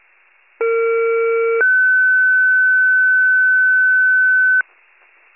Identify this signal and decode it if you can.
Two-tone page, heard as a one-second tone, then a three-second tone, about 460 then 1500 Hz. It is heard from 0.6 to 4.6 s.